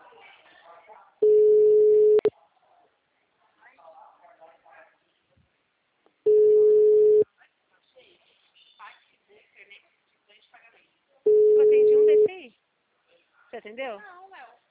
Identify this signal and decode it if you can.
Ringback tone (Europe, 425 Hz), with speech.